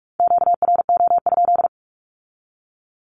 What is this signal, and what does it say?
Morse code, heard as a keyed tone, about 46 wpm, 700 Hz. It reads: QRO?
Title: Q-Codes with Spoken Meaning - 45wpm